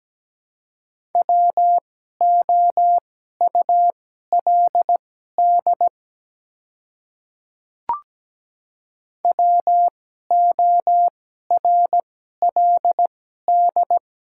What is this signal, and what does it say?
Morse code, heard as a keyed tone, about 17 wpm, spaced as for 12 wpm, 700 Hz. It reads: WOULD WORLD
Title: 2x Character Spacing - Top 200 Words - 17wpm